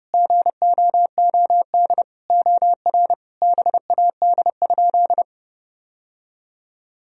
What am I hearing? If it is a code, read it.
Morse code, heard as a keyed tone, about 30 wpm, 700 Hz. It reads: GOOD OR BAD?
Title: Sentences from Top 1000 Words - Review Part 1 - 30wpm